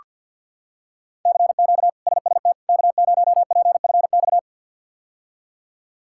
Morse code, heard as a keyed tone, about 46 wpm, 700 Hz. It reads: CQ SST K0PFX
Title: K1USN SST QSO Practice - 50wpm